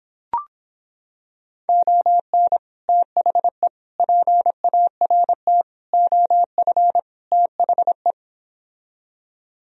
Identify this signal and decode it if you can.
Morse code, heard as a keyed tone, about 26 wpm, 700 Hz. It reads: ON THE PART OF THE